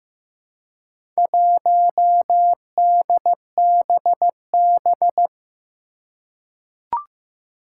Morse code, heard as a keyed tone, about 15 wpm, 700 Hz. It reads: 1DBB
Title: Four Challenging Letter-Number Combinations 15wpm